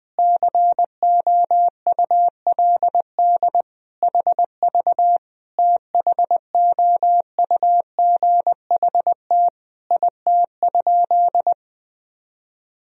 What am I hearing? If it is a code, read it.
Morse code, heard as a keyed tone, about 20 wpm, 700 Hz. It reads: COULD HV THOUGHT IT?
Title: Sentences from Top 300 Words and Abbreviations 20wpm